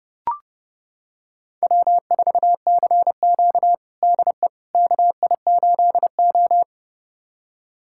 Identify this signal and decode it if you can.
Morse code, heard as a keyed tone, about 29 wpm, 700 Hz. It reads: W4CQ DE KI8O
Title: USA Callsign DE Callsign - 4 Characters - 30wpm